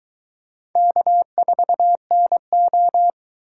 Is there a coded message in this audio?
Yes: Morse code from 0.8 to 3.1 s, heard as a keyed tone, about 22 wpm, 700 Hz. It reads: K4NO